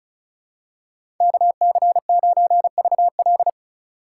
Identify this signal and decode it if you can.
Morse code, heard as a keyed tone, about 34 wpm, 700 Hz. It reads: KC9VL
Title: USA Call Signs - 6x Character Spacing - 35wpm